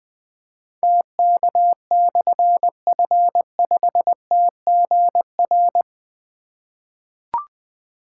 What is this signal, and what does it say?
Morse code, heard as a keyed tone, about 20 wpm, 700 Hz. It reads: TK/F5TGR
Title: Difficult Call Signs 20wpm